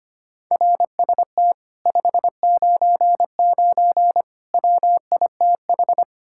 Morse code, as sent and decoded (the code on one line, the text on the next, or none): .-. ... - / ..... ----. ----. / .-- .. - ....
RST 599 WITH